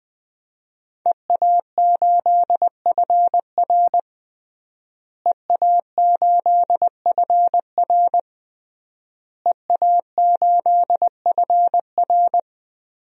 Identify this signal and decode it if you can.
Morse code, heard as a keyed tone, about 21 wpm, 700 Hz. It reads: EA8FR EA8FR EA8FR
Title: DXCC-100 Call Signs - 5 Characters - Repeated 3x - 20wpm